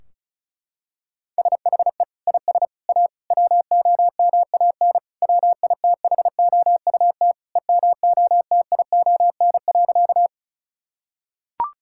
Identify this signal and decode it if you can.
Morse code, heard as a keyed tone, about 34 wpm, 700 Hz. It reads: SHE IS A WOMAN WITHOUT EMOTION.
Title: Sentences from Top 1600 Words 35wpm